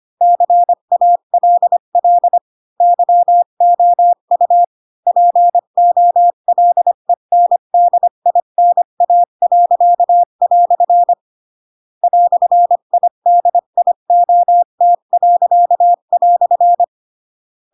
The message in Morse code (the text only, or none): CALL YOU POLENDINA." "IDIOT."